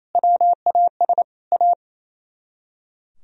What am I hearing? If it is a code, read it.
Morse code, heard as a keyed tone, about 33 wpm, 700 Hz. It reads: WAS A
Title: Speed-Racing - Sets of 3 Words 28wpm